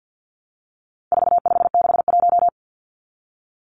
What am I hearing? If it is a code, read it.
Morse code, heard as a keyed tone, about 48 wpm, 700 Hz. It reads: 4561
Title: Numbers - 4-Digit 50wpm